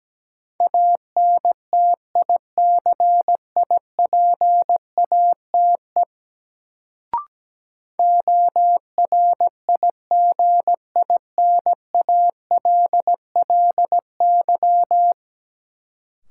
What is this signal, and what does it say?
Morse code, heard as a keyed tone, about 17 wpm, 700 Hz. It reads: ANTICIPATE ORIGINALLY
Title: Sentences from Top 1500 Words - New Word Drill - 17wpm